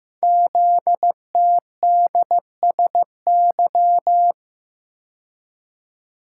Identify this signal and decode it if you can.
Morse code, heard as a keyed tone, about 15 wpm, 700 Hz. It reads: ZTDSY
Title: Code Groups - 5 Letters - 15wpm